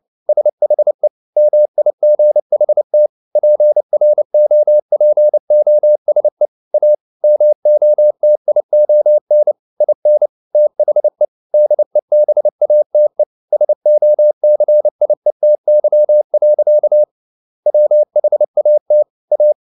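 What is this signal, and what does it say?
Morse code, heard as a keyed tone, about 29 wpm, 600 Hz. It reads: SHE MIGHT PROPOSE A MOTION IN THE DEBATE SOCIETY. WHAT A
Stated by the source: Morse code at 29 wpm, sending Ken Follett's novel Never in English, from CW med Gnister.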